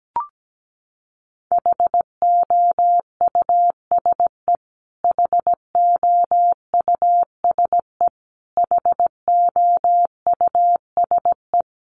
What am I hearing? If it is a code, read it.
Morse code, heard as a keyed tone, about 17 wpm, 700 Hz. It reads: HOUSE HOUSE HOUSE
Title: Top 200 Words Repeated 17wpm